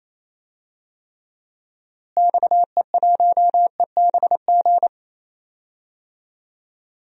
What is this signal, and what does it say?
Morse code, heard as a keyed tone, about 27 wpm, 700 Hz. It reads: XE1EBG
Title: DXCC-100 Call Signs - 6 Characters - Repeated 2x - 28wpm